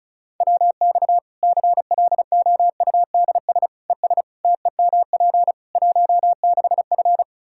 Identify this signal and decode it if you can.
Morse code, heard as a keyed tone, about 34 wpm, 700 Hz. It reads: WX CLOUDS ES TEMP 16F
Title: Practice QSO - Exchange 2 - Info - 35wpm